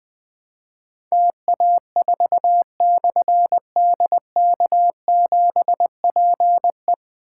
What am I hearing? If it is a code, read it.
Morse code, heard as a keyed tone, about 21 wpm, 700 Hz. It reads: TA4/DK7PE
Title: Difficult Call Signs 20wpm